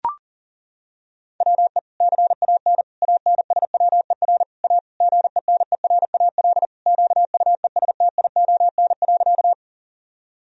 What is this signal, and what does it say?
Morse code, heard as a keyed tone, about 40 wpm, 700 Hz. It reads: WE CAN ANSWER A GENERAL QUESTION.
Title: Sentences from Top 1000 Words - Review Part 3 - 40wpm